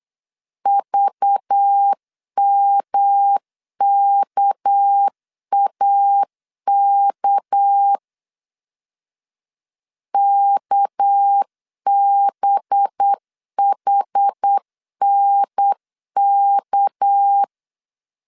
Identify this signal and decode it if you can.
Morse code, heard as a keyed tone, about 8 wpm, 800 Hz. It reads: VMKAK KBHNK
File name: LEZIONI / 6 PKX0 / 6 PK 8wpm 3003